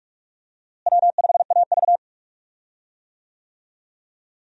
Morse code, heard as a keyed tone, about 42 wpm, 700 Hz. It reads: W5AV
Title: USA Call Signs - 4 Characters - Repeated 3x - 45wpm